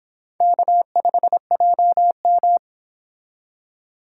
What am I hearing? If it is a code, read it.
Morse code, heard as a keyed tone, about 25 wpm, 700 Hz. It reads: K5JM